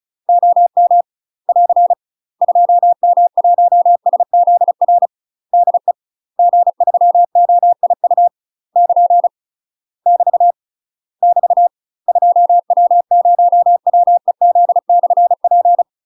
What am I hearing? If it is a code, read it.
Morse code, heard as a keyed tone, about 36 wpm, 700 Hz. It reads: OM <AR> 2M1SZR DE G3OIU <KN> <BT> <BT> 2W0WEZ/P
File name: UK MASTER QSOs TRAINING / 35WPM / UK MASTER QSOs 0002